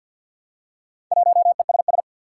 Morse code, heard as a keyed tone, about 52 wpm, 700 Hz. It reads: 1ESS
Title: Four Challenging Letter-Number Combinations 50wpm